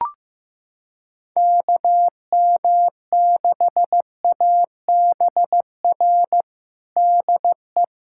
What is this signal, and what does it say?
Morse code, heard as a keyed tone, about 15 wpm, 700 Hz. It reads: KM6ABR DE
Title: USA Callsign DE Callsign 15wpm